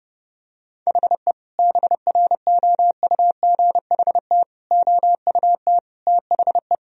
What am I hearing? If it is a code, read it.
Morse code, heard as a keyed tone, about 30 wpm, 700 Hz. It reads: HE BROUGHT OUT THE
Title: Sentences from Top 1000 Words - Review Part 3 - 30wpm